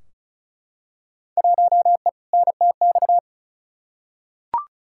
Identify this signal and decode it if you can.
Morse code, heard as a keyed tone, about 36 wpm, 700 Hz. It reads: 1E NTX
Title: Field Day Exchanges - 35wpm